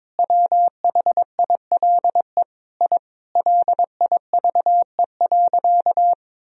Morse code, heard as a keyed tone, about 22 wpm, 700 Hz. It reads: WHILE I LIVE.
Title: Sentences from Top 1000 Words - Review Part 1 - 22wpm